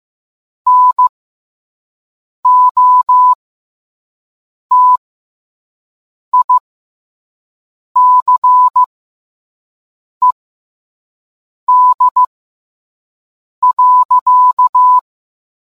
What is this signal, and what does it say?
Morse code, heard as a keyed tone, about 15 wpm, spaced as for 5 wpm, 1000 Hz. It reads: NOTICED.